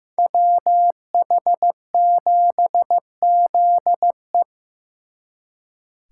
Morse code, heard as a keyed tone, about 18 wpm, 700 Hz. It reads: WH7ZE
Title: Speed Racing - USA 2x2 Call Signs 15wpm